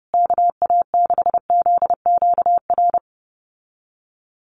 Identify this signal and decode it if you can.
Morse code, heard as a keyed tone, about 28 wpm, 700 Hz. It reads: KA6ZQR
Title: USA Call Signs Repeated 3x 30wpm